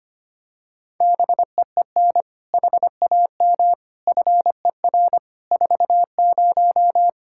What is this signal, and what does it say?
Morse code, heard as a keyed tone, about 25 wpm, 700 Hz. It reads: BEEN HAM FER 40
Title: Practice QSO - Exchange 3 - More Info - 25wpm